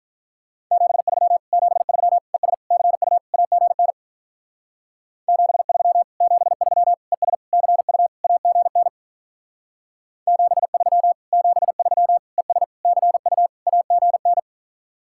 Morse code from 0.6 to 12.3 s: 73 73 ES CU AGN 73 73 ES CU AGN 73 73

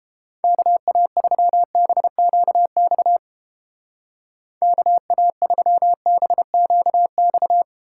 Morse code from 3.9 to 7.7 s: KA3BQX

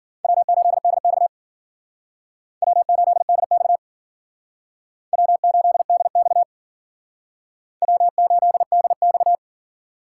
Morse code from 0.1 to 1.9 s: W8DX

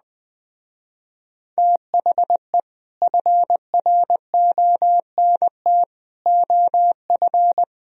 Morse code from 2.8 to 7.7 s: FRONT OF

